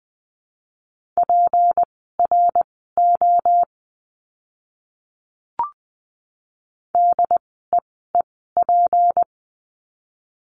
Morse code, heard as a keyed tone, about 20 wpm, spaced as for 14 wpm, 700 Hz. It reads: PRO DEEP
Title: 153 Lesson 25f - Letter p - Words 20wpm